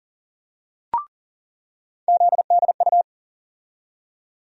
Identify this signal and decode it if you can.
Morse code, heard as a keyed tone, about 40 wpm, 700 Hz. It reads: ZDU
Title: USA Call Sign Postfixes - 3 Letters 40wpm